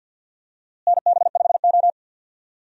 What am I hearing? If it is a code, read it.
Morse code, heard as a keyed tone, about 53 wpm, 700 Hz. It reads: NB5O
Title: USA Call Signs - 4 Characters - Repeated 3x - 50wpm